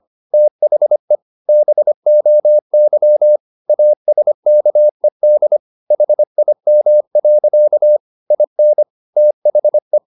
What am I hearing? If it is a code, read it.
Morse code, heard as a keyed tone, about 25 wpm, 600 Hz. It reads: THE BOY ASKED HIM. IN THE
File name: the old man and the sea part 1 25wpm